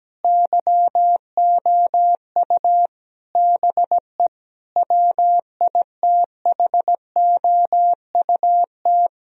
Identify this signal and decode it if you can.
Morse code, heard as a keyed tone, about 17 wpm, 700 Hz. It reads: YOU BE WITHOUT
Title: Sentences from Top 1000 Words - Review Part 3 - 17wpm